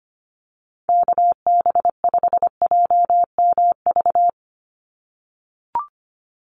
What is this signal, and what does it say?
Morse code, heard as a keyed tone, about 25 wpm, 700 Hz. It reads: KB5JMV
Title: USA Call Signs 25wpm